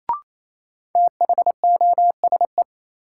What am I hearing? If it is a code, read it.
Morse code, heard as a keyed tone, about 28 wpm, 700 Hz. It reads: THOSE
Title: Sentences from Top 300 Words - New Word Drill - 28wpm